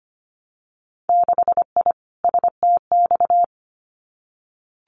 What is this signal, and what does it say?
Morse code, heard as a keyed tone, about 25 wpm, 690 Hz. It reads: 6I STX
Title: Winter Field Day Exchanges Encore 25wpm